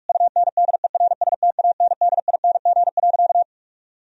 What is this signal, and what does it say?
Morse code, heard as a keyed tone, about 46 wpm, 700 Hz. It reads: UNDERSTANDING.
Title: Sentences from Top 1000 Words - Review Part 3 - 45wpm